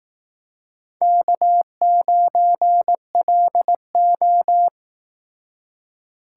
Morse code, heard as a keyed tone, about 19 wpm, 700 Hz. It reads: K9LO